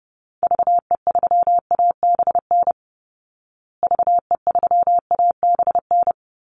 Morse code, heard as a keyed tone, about 30 wpm, 700 Hz. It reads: VE3ABN VE3ABN
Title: Call Signs Repeated 2x - 30wpm